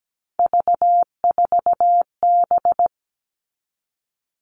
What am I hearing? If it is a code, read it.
Morse code, heard as a keyed tone, about 17 wpm, 700 Hz. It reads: V4B
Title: All Cut Numbers - 3-Digit 17wpm